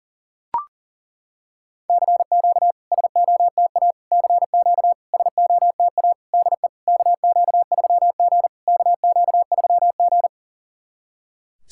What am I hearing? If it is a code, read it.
Morse code, heard as a keyed tone, about 40 wpm, 700 Hz. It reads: CQ SOTA CQ SOTA DE KQ3G KQ3G
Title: SOTA QSO Practice 40wpm